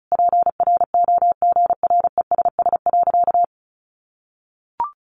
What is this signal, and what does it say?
Morse code, heard as a keyed tone, about 34 wpm, 700 Hz. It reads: PROGRESS.
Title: Sentences from Top 1400 Words 35wpm